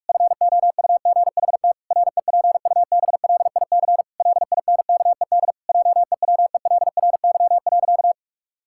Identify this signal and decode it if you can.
Morse code, heard as a keyed tone, about 41 wpm, 700 Hz. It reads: FOUGHT REPUBLIC LINKED JEWELRY.
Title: Pangram Challenge v2 - 45wpm